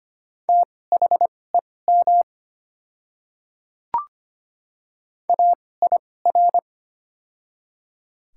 Morse code, heard as a keyed tone, about 25 wpm, spaced as for 18 wpm, 690 Hz. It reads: THEM AIR
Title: 2x Character Spacing - Top 200 Words - 25wpm